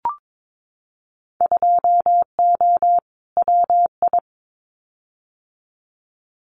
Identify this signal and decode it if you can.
Morse code, heard as a keyed tone, about 22 wpm, 700 Hz. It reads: 2O WI
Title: Winter Field Day Exchanges 22wpm